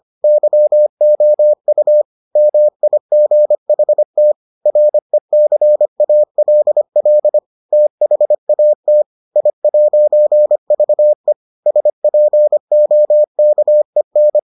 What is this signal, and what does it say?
Morse code, heard as a keyed tone, about 25 wpm, 600 Hz. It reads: YOU MIGHT RECALL THAT I'VE SPOKEN